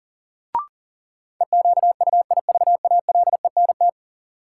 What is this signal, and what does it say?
Morse code, heard as a keyed tone, about 40 wpm, 700 Hz. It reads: EQUIVALENT
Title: Sentences from Top 1600 Words - New Word Drill Alt - 40wpm